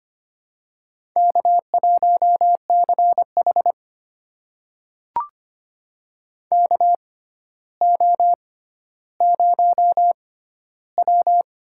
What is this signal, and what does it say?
Morse code, heard as a keyed tone, about 25 wpm, spaced as for 13 wpm, 700 Hz. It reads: K1CH KO0W
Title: USA Call Signs - 4 Characters - 6x Character Spacing - 25wpm